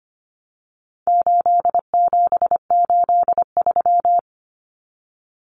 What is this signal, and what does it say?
Morse code, heard as a keyed tone, about 25 wpm, 700 Hz. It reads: 8783
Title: Numbers - 4-Digit 25wpm